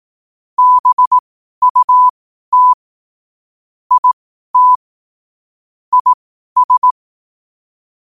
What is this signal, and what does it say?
Morse code, heard as a keyed tone, about 18 wpm, spaced as for 12 wpm, 1000 Hz. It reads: BUT IT IS